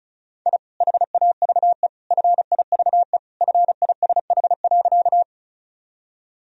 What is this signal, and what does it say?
Morse code, heard as a keyed tone, about 34 wpm, 700 Hz. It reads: I HAVE FIVE FISH.
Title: Sentences from Top 1000 Words - Review Part 2 - 35wpm